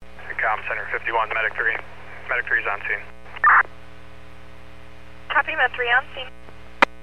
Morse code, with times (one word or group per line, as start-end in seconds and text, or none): none